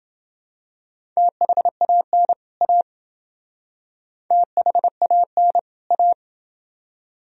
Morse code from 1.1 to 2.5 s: THAN

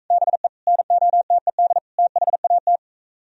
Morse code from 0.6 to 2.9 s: NOTED THAT

